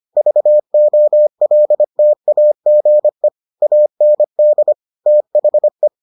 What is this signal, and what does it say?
Morse code, heard as a keyed tone, about 25 wpm, 600 Hz. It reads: VOLTAGE AND THE